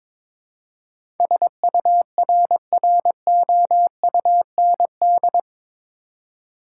Morse code, heard as a keyed tone, about 22 wpm, 700 Hz. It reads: SURROUND
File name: practice / 22wpm / Eight Letter Words - 22wpm